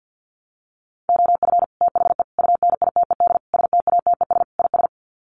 Morse code, heard as a keyed tone, about 50 wpm, 700 Hz. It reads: OF THE UNITED STATES IS
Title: Sets of 5 Words Encore2 50wpm